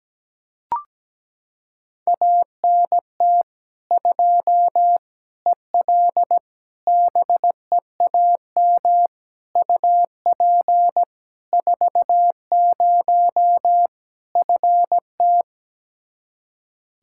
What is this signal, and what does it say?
Morse code, heard as a keyed tone, about 17 wpm, 700 Hz. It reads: ANT 2 EL BEAM UP 40 FT
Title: K1USN SST QSO Practice 17wpm